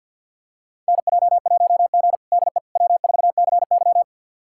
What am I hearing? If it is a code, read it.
Morse code, heard as a keyed tone, about 53 wpm, 700 Hz. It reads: NJ1G DE W4CY
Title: USA Callsign DE Callsign - 4 Characters - 50wpm